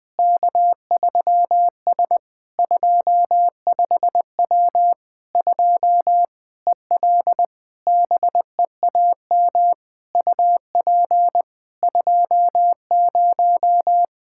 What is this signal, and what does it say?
Morse code, heard as a keyed tone, about 20 wpm, 700 Hz. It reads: K3S 25W 2 EL BEAM UP 20
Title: Practice QSO - Exchange 2 - Info - 20wpm